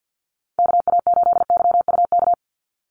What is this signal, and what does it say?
Morse code, heard as a keyed tone, about 50 wpm, 700 Hz. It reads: KA8YUK
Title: USA Call Signs 50wpm